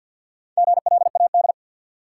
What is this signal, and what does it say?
Morse code, heard as a keyed tone, about 53 wpm, 700 Hz. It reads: GLAD
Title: Sentences from Top 800 Words - New Word Drill - 50wpm